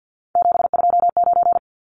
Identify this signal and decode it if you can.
Morse code, heard as a keyed tone, about 50 wpm, 700 Hz. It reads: 729
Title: Numbers - 3-Digit 50wpm